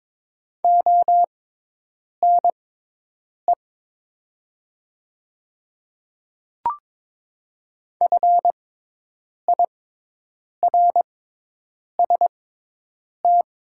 Morse code, heard as a keyed tone, about 22 wpm, spaced as for 8 wpm, 700 Hz. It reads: ONE FIRST
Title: 6x Character Spacing - Top 100 Words - 22wpm